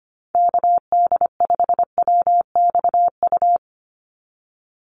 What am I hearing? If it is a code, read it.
Morse code, heard as a keyed tone, about 25 wpm, 700 Hz. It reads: KD5WXU
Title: USA Call Signs 25wpm